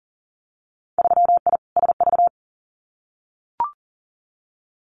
Morse code, heard as a keyed tone, about 41 wpm, 700 Hz. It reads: 3I SV
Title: Winter Field Day Exchanges Encore 40wpm